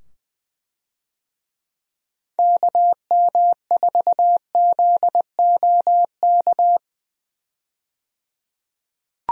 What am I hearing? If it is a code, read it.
Morse code, heard as a keyed tone, about 21 wpm, 700 Hz. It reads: KM4ZOK
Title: USA Call Signs - 6 Characters - 20wpm